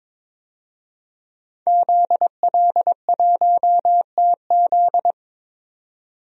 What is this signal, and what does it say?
Morse code, heard as a keyed tone, about 22 wpm, 700 Hz. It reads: ZL1TZ
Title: DXCC-100 Call Signs - 5 Characters - 22wpm